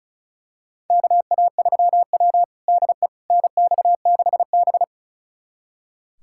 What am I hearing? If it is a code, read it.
Morse code, heard as a keyed tone, about 36 wpm, 700 Hz. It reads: KA3W DE NX6B
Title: USA Callsign DE Callsign - 4 Characters - 35wpm